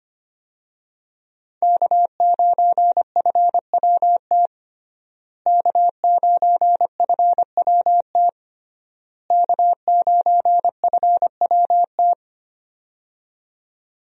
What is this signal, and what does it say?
Morse code, heard as a keyed tone, about 25 wpm, 700 Hz. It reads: K9FWT K9FWT K9FWT
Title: USA Call Signs - 5 Characters - Repeated 3x - 25wpm